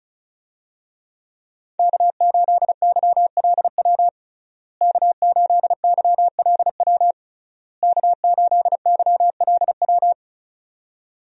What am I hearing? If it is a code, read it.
Morse code, heard as a keyed tone, about 36 wpm, 700 Hz. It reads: K8YLW K8YLW K8YLW
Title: USA Call Signs - 5 Characters - Repeated 3x - 35wpm